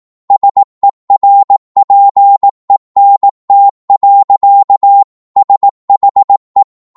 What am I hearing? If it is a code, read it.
Morse code, heard as a keyed tone, about 18 wpm, 800 Hz. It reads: SERPENT. SHE
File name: 18wpm